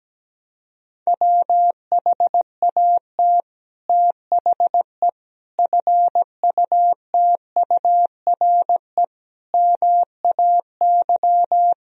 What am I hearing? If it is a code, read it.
Morse code, heard as a keyed tone, about 17 wpm, 700 Hz. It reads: WHAT THE FUTURE MAY
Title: Sentences from Top 1000 Words - Review Part 2 - 17wpm